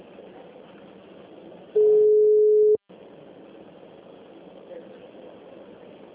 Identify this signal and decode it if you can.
Ringback tone (Europe, 425 Hz).